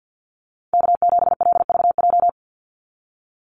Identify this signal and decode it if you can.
Morse code, heard as a keyed tone, about 49 wpm, 700 Hz. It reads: K7LVJ
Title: Call Signs - 50wpm